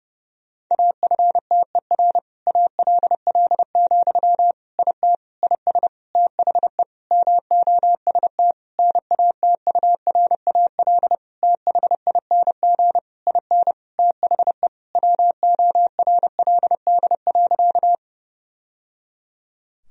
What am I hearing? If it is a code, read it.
Morse code, heard as a keyed tone, about 30 wpm, 700 Hz. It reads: AFTER ALL, IT IS THE MOST NATURAL THING IN THE WORLD.
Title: Sentences from Top 1000 Words - Review Part 4 - 30wpm